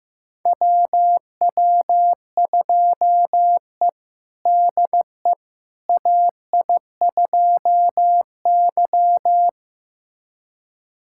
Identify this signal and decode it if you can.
Morse code, heard as a keyed tone, about 15 wpm, 700 Hz. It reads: WW2E DE AI2Y
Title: USA Callsign DE Callsign - 4 Characters - 15wpm